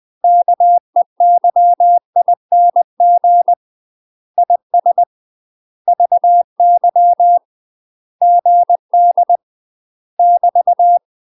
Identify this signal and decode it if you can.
Morse code, heard as a keyed tone, about 20 wpm, 700 Hz. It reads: KEYING IS VY GD <BT>
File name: UK MASTER QSOs TRAINING/20WPM